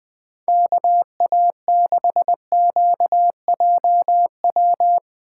Morse code, heard as a keyed tone, about 20 wpm, 700 Hz. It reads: KA6QJW
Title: USA Call Signs - 3x Character Spacing - 20wpm